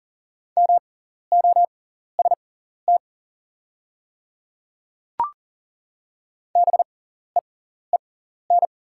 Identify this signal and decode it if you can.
Morse code, heard as a keyed tone, about 38 wpm, spaced as for 14 wpm, 700 Hz. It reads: MOST BEEN
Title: 6x Character Spacing - Top 100 Words - 40wpm